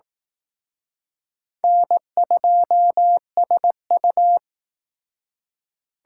Morse code, heard as a keyed tone, about 18 wpm, 700 Hz. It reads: N2SU
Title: Speed Racing - USA Call Signs 15wpm